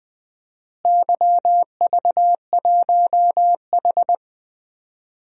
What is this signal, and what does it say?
Morse code, heard as a keyed tone, about 20 wpm, 700 Hz. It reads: YV1H
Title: DXCC-100 Call Signs - 4 Characters - Repeated 2x - 20wpm